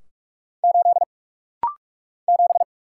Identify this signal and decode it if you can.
Morse code, heard as a keyed tone, about 46 wpm, 700 Hz. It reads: 8 7
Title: Single Letter-Number - Rapid-Fire 45wpm